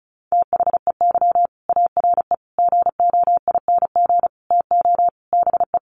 Morse code, heard as a keyed tone, about 34 wpm, 700 Hz. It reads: THEY ARE GOING TO BE
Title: Sets of 5 Words Encore2 35wpm